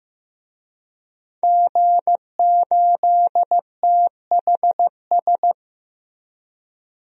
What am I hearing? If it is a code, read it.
Morse code, heard as a keyed tone, about 15 wpm, 700 Hz. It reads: G8THS
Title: DXCC-100 Call Signs - 5 Characters - 15wpm